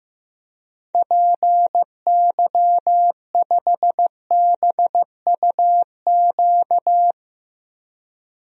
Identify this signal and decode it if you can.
Morse code, heard as a keyed tone, about 15 wpm, 700 Hz. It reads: PY5BUQ
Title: DXCC-100 Call Signs - 6 Characters - Repeated 3x - 15wpm